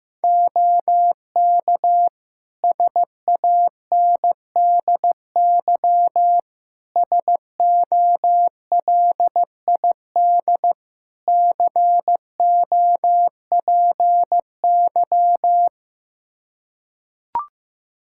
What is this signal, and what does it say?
Morse code, heard as a keyed tone, about 15 wpm, 700 Hz. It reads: OK SANDY SOLID COPY
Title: Practice QSO - Exchange 3 - More Info - 15wpm